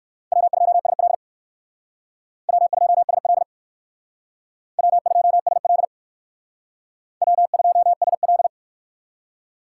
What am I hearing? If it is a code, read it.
Morse code, heard as a keyed tone, about 57 wpm, 700 Hz. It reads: W2SL W2SL W2SL W2SL